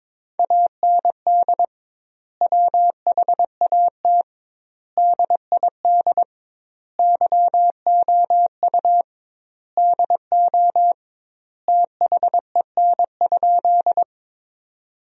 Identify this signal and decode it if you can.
Morse code, heard as a keyed tone, about 23 wpm, 700 Hz. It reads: AND WHAT DID YOU DO THEN?
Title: 2x Word-Spacing - Sentences from Top 100 Words - 22wpm